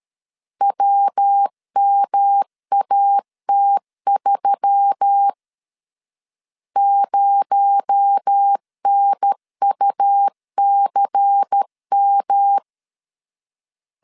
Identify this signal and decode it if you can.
Morse code, heard as a keyed tone, about 13 wpm, 800 Hz. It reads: WMAT3 0NUCM
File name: CW / DET / 1 6 1500C S B 12wpm 3431